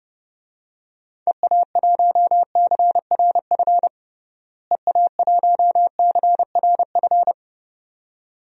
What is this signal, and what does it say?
Morse code, heard as a keyed tone, about 30 wpm, 700 Hz. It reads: EA1CRF EA1CRF
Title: DXCC-100 Call Signs - 6 Characters - Repeated 2x - 30wpm